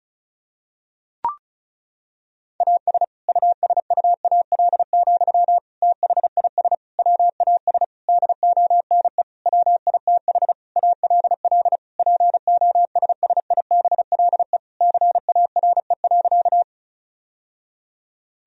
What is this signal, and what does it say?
Morse code, heard as a keyed tone, about 34 wpm, 700 Hz. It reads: AS USUAL, THIS WAS DONE WITH ALL POSSIBLE CARE.
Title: Sentences from Top 1000 Words - Review Part 2 - 35wpm